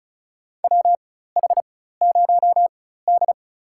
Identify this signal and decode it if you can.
Morse code, heard as a keyed tone, about 34 wpm, spaced as for 16 wpm, 700 Hz. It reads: WH0D
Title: USA Call Signs - 4 Characters - 4x Character Spacing - 35wpm